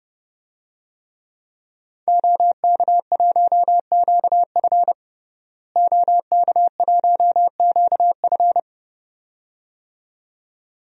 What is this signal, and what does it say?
Morse code, heard as a keyed tone, about 32 wpm, 700 Hz. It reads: OK1QF OK1QF
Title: DXCC-100 Call Signs - 5 Characters - Repeated 2x - 30wpm